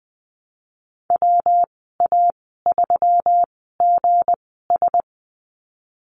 Morse code, heard as a keyed tone, about 20 wpm, spaced as for 14 wpm, 700 Hz. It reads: WA3GS
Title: 193 Lesson 31g - Letter v - Call Signs 20wpm